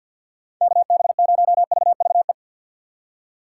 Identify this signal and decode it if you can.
Morse code, heard as a keyed tone, about 51 wpm, 700 Hz. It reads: KB0VVE
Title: USA Call Signs - 6 Characters - 4x Character Spacing - 50wpm